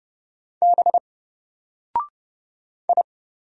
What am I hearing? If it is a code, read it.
Morse code, heard as a keyed tone, about 30 wpm, 700 Hz. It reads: B I
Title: Single Character (Letter:Number:Punctuation:Prosign) Alt 30wpm